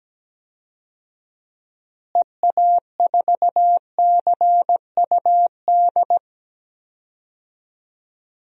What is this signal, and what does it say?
Morse code, heard as a keyed tone, about 17 wpm, 700 Hz. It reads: EA4CUD
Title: DXCC-100 Call Signs - 6 Characters - Repeated 2x - 17wpm